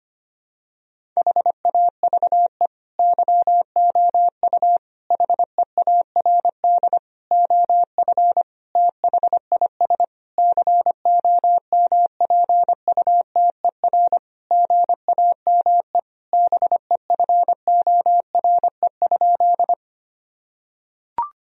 Morse code, heard as a keyed tone, about 25 wpm, 700 Hz. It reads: HAVE YOU HEARD OF THIS COMPUTER GAME BEFORE?
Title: Sentences from Top 1000 Words - Review Part 3 - 25wpm